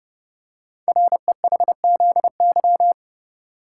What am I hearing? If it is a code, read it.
Morse code, heard as a keyed tone, about 30 wpm, 700 Hz. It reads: REHZY
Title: Code Groups - 5 Letters - 30wpm